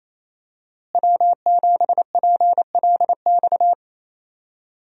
Morse code, heard as a keyed tone, about 28 wpm, 700 Hz. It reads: W7PLX
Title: USA Call Signs - 5 Characters - Repeated 3x - 28wpm